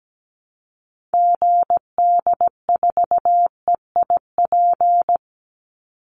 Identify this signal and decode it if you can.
Morse code, heard as a keyed tone, about 17 wpm, 700 Hz. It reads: GD4EIP
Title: Call Signs Repeated 3x - 17wpm